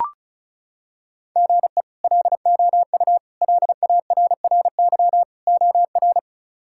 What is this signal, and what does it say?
Morse code, heard as a keyed tone, about 36 wpm, 700 Hz. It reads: GE LOU LARRY OR
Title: K1USN SST QSO Practice - 35wpm